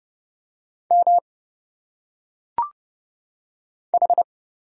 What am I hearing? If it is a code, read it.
Morse code, heard as a keyed tone, about 30 wpm, 700 Hz. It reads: M H
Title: Single Letters - Phonetic Pronunciation 30wpm